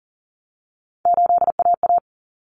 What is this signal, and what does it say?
Morse code, heard as a keyed tone, about 40 wpm, 700 Hz. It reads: 8AA
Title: Common Cut Numbers - 3-Digit 40wpm